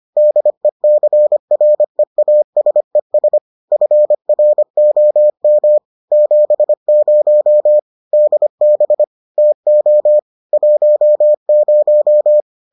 Morse code, heard as a keyed tone, about 25 wpm, 600 Hz. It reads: DECREASES FROM 70 DB TO 10